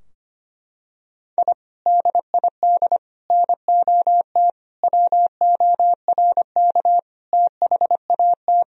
Morse code, heard as a keyed tone, about 25 wpm, 700 Hz. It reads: I DID NOT WORK THAT
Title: Sentences from Top 1000 Words - Review Part 2 - 25wpm